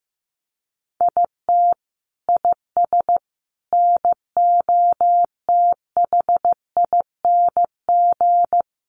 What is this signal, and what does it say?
Morse code, heard as a keyed tone, about 15 wpm, 700 Hz. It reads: IT IS NOTHING
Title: Sentences from Top 1100 Words 15wpm